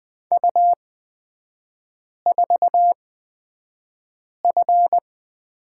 Morse code, heard as a keyed tone, about 20 wpm, spaced as for 5 wpm, 700 Hz. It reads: U4F
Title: 243 Lesson 39e - Letter z - Three Characters 20wpm